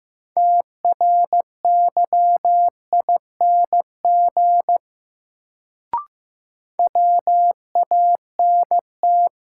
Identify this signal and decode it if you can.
Morse code, heard as a keyed tone, about 15 wpm, 700 Hz. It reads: TRYING WANT
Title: Sets of 2 Words - New Word Drill - 15wpm